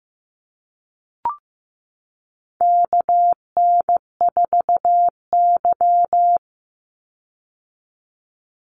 Morse code, heard as a keyed tone, about 15 wpm, 700 Hz. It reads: KN4Y
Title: Call Signs Repeated 2x - 15wpm